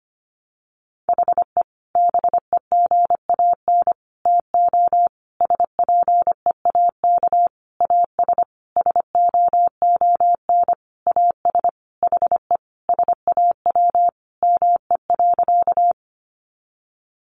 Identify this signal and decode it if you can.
Morse code, heard as a keyed tone, about 25 wpm, 700 Hz. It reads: HE BEGAN TO SPEAK AS SOON AS HE SAW ME.
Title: Sentences from Top 1100 Words 25wpm